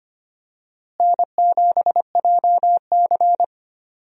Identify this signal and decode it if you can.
Morse code, heard as a keyed tone, about 25 wpm, 700 Hz. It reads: N7JC
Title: USA Call Signs - 4 Characters - Repeated 2x - 25wpm